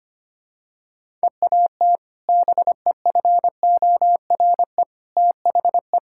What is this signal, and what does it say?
Morse code, heard as a keyed tone, about 25 wpm, 700 Hz. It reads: EAT BEFORE THE
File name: practice / 25wpm / Sentences with 2-Word Phrases - 25wpm